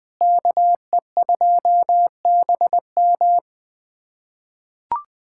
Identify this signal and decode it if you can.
Morse code, heard as a keyed tone, about 21 wpm, 700 Hz. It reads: KE2BM
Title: USA Callsign DE Callsign - 5 Characters - 20wpm